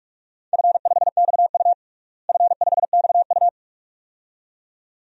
Morse code, heard as a keyed tone, about 46 wpm, 700 Hz. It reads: F5XU F5XU
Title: DXCC-100 Call Signs - 4 Characters - Repeated 2x - 45wpm